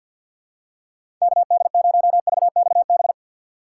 Morse code, heard as a keyed tone, about 48 wpm, 700 Hz. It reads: KD0VXB
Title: USA Call Signs - 6 Characters - 3x Character Spacing - 50wpm